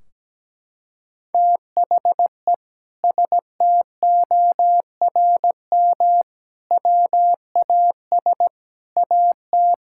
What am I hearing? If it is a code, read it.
Morse code, heard as a keyed tone, about 17 wpm, 700 Hz. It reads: THE STORM WAS AT
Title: Sentences from Top 1600 Words 17wpm